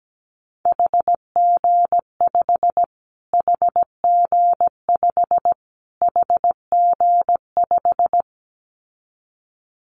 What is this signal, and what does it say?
Morse code, heard as a keyed tone, about 17 wpm, 700 Hz. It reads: HG5 HG5 HG5
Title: Callsign Prefixes for Top 25 DXCC - Repeated 3x - 17wpm